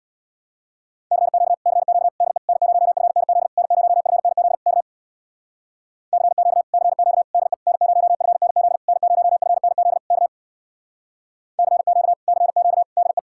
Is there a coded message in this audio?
Yes: Morse code from 1.1 to 13.2 s, heard as a keyed tone, about 64 wpm, 700 Hz. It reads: CQ CQ DE N0FNZ N0FNZ K CQ CQ DE N0FNZ N0FNZ K CQ CQ DE